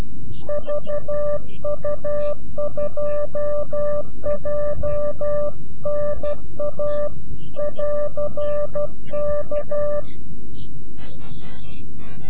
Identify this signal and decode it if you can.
Morse code, heard as a keyed tone, about 12 wpm, 590 Hz. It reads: VU2JNA <AR>K
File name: CW-QSO / CQ Call / CQ-QSO